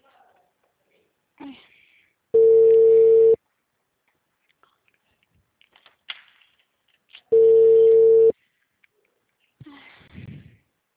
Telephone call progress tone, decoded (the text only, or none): ringback tone